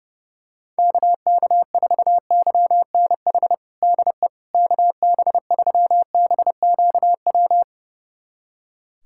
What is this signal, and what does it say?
Morse code, heard as a keyed tone, about 31 wpm, 700 Hz. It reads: KK4YNH DE KB3BQW
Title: USA Callsign DE Callsign - 6 Characters - 30wpm